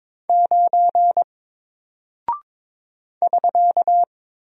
Morse code, heard as a keyed tone, about 22 wpm, 700 Hz. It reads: 9 <SK>
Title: Single Character (Letter:Number:Punctuation:Prosign) 22wpm